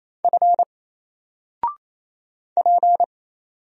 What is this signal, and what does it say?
Morse code, heard as a keyed tone, about 28 wpm, 700 Hz. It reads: F P
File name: Single-Letter-Number-28wpm